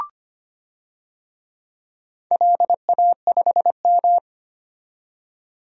Morse code, heard as a keyed tone, about 25 wpm, 700 Hz. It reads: LA5M
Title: DXCC-100 Call Signs - 4 Characters - 25wpm